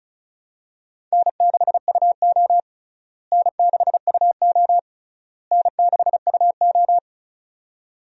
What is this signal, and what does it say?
Morse code, heard as a keyed tone, about 34 wpm, 700 Hz. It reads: N6UO N6UO N6UO
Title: USA Call Signs - 4 Characters - Repeated 3x - 35wpm